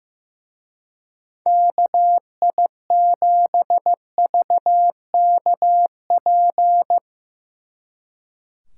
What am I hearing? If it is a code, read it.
Morse code, heard as a keyed tone, about 15 wpm, 700 Hz. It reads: KI7VKP
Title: Speed Racing - USA 2x3 Call Signs 15wpm